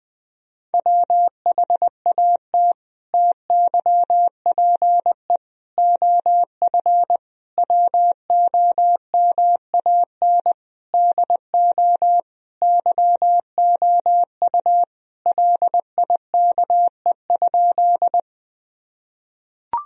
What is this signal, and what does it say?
Morse code, heard as a keyed tone, about 20 wpm, 700 Hz. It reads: WHAT TYPE OF WOMAN DO YOU LIKE?
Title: Sentences from Top 1000 Words - Review Part 3 - 20wpm